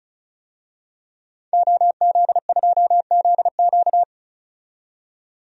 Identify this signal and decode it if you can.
Morse code, heard as a keyed tone, about 34 wpm, 700 Hz. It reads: OZ2ZQ
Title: DXCC-100 Call Signs - 5 Characters - 35wpm